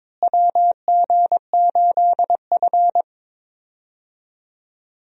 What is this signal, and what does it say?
Morse code, heard as a keyed tone, about 22 wpm, 700 Hz. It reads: WG8F
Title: USA Call Signs - 4 Characters - Repeated 2x - 22wpm